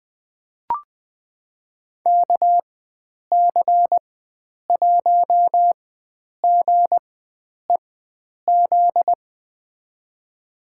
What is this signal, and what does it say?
Morse code, heard as a keyed tone, about 20 wpm, spaced as for 9 wpm, 700 Hz. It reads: KC1GEZ
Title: USA Call Signs - 4x Character Spacing - 20wpm